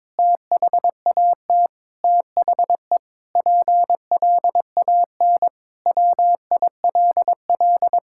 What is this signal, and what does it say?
Morse code, heard as a keyed tone, about 22 wpm, 700 Hz. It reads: THAT THE PLAN WILL
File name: Sentences with 7-Word Phrases - 22wpm